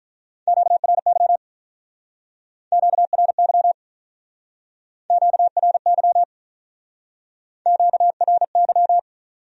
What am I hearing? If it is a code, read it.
Morse code, heard as a keyed tone, about 44 wpm, 700 Hz. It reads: QRY QRY QRY QRY